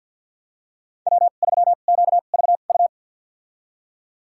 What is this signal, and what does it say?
Morse code, heard as a keyed tone, about 52 wpm, spaced as for 37 wpm, 690 Hz. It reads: W3QVU
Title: USA Call Signs - 5 Characters - 2x Character Spacing - 50wpm